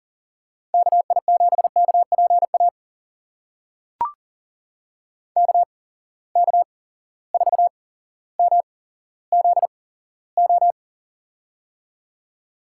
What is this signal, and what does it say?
Morse code, heard as a keyed tone, about 39 wpm, spaced as for 17 wpm, 700 Hz. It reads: KI7KPA KK4MZO
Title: USA Call Signs - 6 Characters - 8x Character Spacing - 40wpm